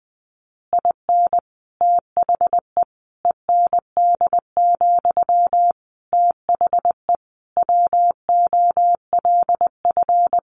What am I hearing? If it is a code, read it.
Morse code, heard as a keyed tone, about 20 wpm, 700 Hz. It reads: IN THE END, THE WOLF
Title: Additional Easy Phrases 20wpm